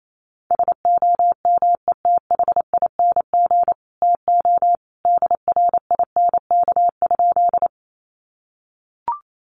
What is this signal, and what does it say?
Morse code, heard as a keyed tone, about 28 wpm, 700 Hz. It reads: SOMETHING TO DRINK?
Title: Sentences from Top 1500 Words 28wpm